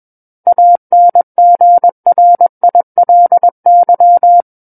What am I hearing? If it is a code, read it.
Morse code, heard as a keyed tone, about 21 wpm, 700 Hz. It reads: ANGRILY
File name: Book One / 12-Chapter 12 - 21 WPM